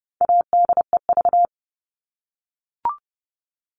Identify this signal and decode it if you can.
Morse code, heard as a keyed tone, about 30 wpm, 700 Hz. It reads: ADEV
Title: All Cut Numbers - 4-Digit 30wpm